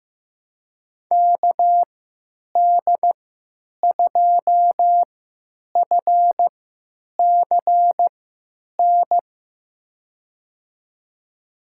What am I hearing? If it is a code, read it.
Morse code, heard as a keyed tone, about 16 wpm, spaced as for 9 wpm, 700 Hz. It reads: KD2FCN
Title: USA Call Signs - 6 Characters - 3x Character Spacing - 15wpm